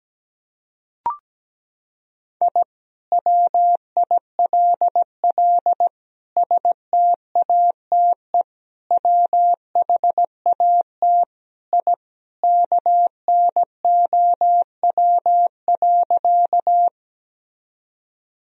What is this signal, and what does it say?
Morse code, heard as a keyed tone, about 17 wpm, 700 Hz. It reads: I WILL STATE WHAT I KNOW.
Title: Sentences from Top 1000 Words - Review Part 2 - 17wpm